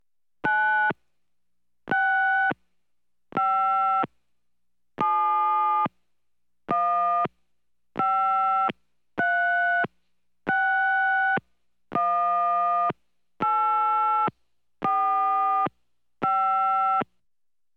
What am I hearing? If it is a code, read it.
DTMF, heard as telephone touch tones, tones of about 740 ms. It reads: 965715361845
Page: telephone-touch-tone-push-internal-office